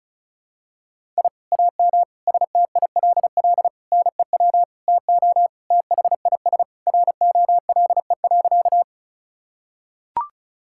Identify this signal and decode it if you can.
Morse code, heard as a keyed tone, about 36 wpm, 700 Hz. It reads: I AM STILL NEW TO THIS ROLE.
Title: Sentences from Top 1000 Words - Review Part 6 - 35wpm